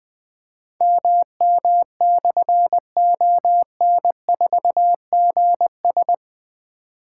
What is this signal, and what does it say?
Morse code, heard as a keyed tone, about 20 wpm, 700 Hz. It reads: MM/ON4GS
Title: Difficult Call Signs 20wpm